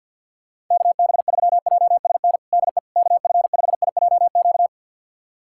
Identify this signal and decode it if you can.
Morse code, heard as a keyed tone, about 52 wpm, 690 Hz. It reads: KB3JSN DE KF5IJQ
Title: USA Callsign DE Callsign - 6 Characters - 50wpm